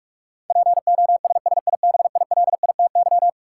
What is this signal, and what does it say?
Morse code, heard as a keyed tone, about 46 wpm, 700 Hz. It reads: POSSIBILITY
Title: Sets of 3 Words - New Word Drill - 45wpm